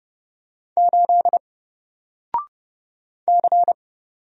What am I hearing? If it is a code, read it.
Morse code, heard as a keyed tone, about 30 wpm, 700 Hz. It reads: 8 C
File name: Single-Letter-Number-30wpm